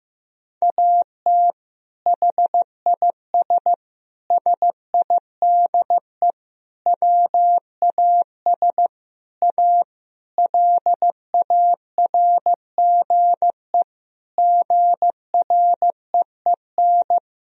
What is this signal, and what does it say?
Morse code, heard as a keyed tone, about 15 wpm, 700 Hz. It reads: AT HIS SIDE WAS A LARGE GREEN
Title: Sentences from Top 1000 Words - Review Part 2 - 15wpm